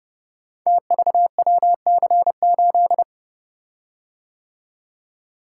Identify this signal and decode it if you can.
Morse code, heard as a keyed tone, about 28 wpm, 700 Hz. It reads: TVWC8
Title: Code Groups - 5 Letter:Numbers - 30wpm